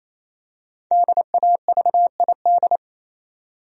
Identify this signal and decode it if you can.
Morse code, heard as a keyed tone, about 28 wpm, 700 Hz. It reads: DAVID
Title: USA Common Names - 28wpm